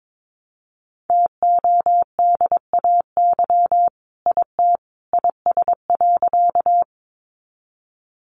Morse code, heard as a keyed tone, about 22 wpm, 700 Hz. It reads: TODAY IT IS.
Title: Sentences from Top 600 Words 22wpm